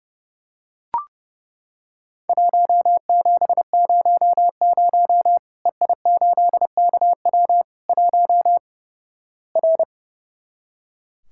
Morse code, heard as a keyed tone, about 30 wpm, 700 Hz. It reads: 1700 EI8KW 1 R
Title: Worked All Europe DX Contest - QTC Exchange - 30wpm